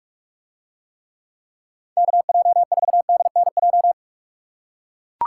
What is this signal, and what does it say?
Morse code, heard as a keyed tone, about 43 wpm, 700 Hz. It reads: KJ4DNJ